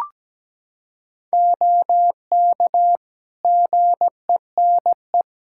Morse code, heard as a keyed tone, about 17 wpm, 700 Hz. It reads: OK GENE
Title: Practice QSO - Exchange 3 - More Info - 17wpm